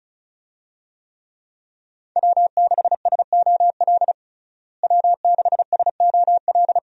Morse code, heard as a keyed tone, about 36 wpm, 700 Hz. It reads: W6SOL W6SOL
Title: USA Call Signs - 5 Characters - Repeated 3x - 35wpm